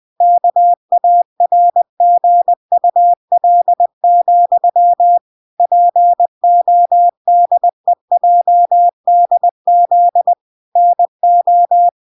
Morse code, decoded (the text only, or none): KARGUL, PODEJDZ NO